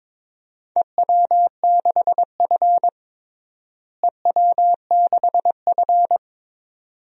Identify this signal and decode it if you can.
Morse code, heard as a keyed tone, about 22 wpm, 700 Hz. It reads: EW6F EW6F
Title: DXCC-100 Call Signs - 4 Characters - Repeated 2x - 22wpm